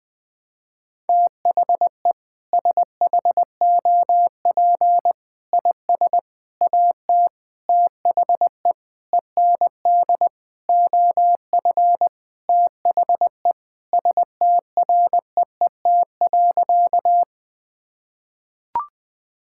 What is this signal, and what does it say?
Morse code, heard as a keyed tone, about 20 wpm, 700 Hz. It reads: THE SHOP IS AT THE END OF THE STREET.
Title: Sentences with 5-Word Phrases - Spoken Beginning - 20wpm